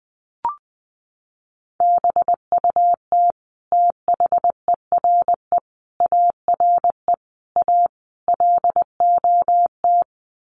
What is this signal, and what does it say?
Morse code, heard as a keyed tone, about 20 wpm, 700 Hz. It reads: BUT THERE ARE A LOT
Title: Sets of 5 Words Encore2 20wpm